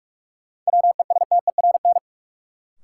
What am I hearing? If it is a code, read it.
Morse code, heard as a keyed tone, about 46 wpm, 700 Hz. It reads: WESTERN
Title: Sentences from Top 1000 Words - Review Part 7 - New Word Drill - 45wpm